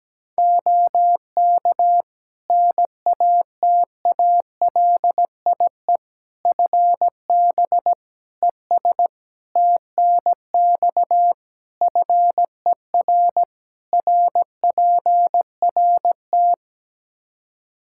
Morse code, heard as a keyed tone, about 17 wpm, 700 Hz. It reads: OK NATALIE FB ES TNX FER RPRT
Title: Practice QSO - Exchange 2 - Info - 17wpm